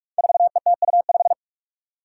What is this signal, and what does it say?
Morse code, heard as a keyed tone, about 44 wpm, 700 Hz. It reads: 4ETU5
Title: Five Challenging Letter-Number Combinations 45wpm